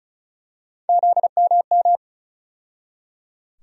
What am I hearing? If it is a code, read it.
Morse code, heard as a keyed tone, about 36 wpm, 700 Hz. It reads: ZMM
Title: USA Call Sign Postfixes - 3 Letters 35wpm